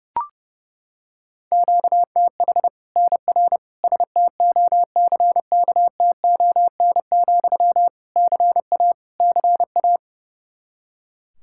Morse code, heard as a keyed tone, about 30 wpm, 700 Hz. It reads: QTH NR STOCKTON, CA CA
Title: Practice QSO - Exchange 1 - Beginning - 30wpm